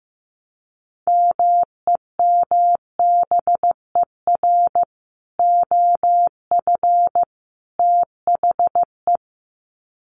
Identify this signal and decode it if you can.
Morse code, heard as a keyed tone, about 15 wpm, 700 Hz. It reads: MEMBER OF THE
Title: Sets of 3 Words 15wpm